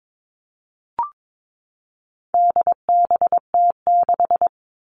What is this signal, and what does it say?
Morse code, heard as a keyed tone, about 22 wpm, 700 Hz. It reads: DBT6
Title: All Cut Numbers - 4-Digit 22wpm